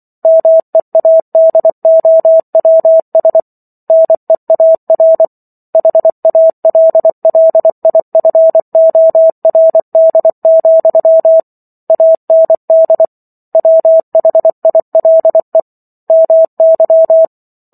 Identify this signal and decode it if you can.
Morse code, heard as a keyed tone, about 24 wpm, 650 Hz. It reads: MEADOWS NEAR HALLIFORD, AND WHILE MY